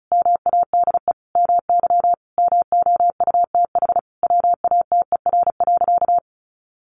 Morse code, heard as a keyed tone, about 34 wpm, 700 Hz. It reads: MADE MY MOUTH WATER.
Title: Sentences from Top 1500 Words 35wpm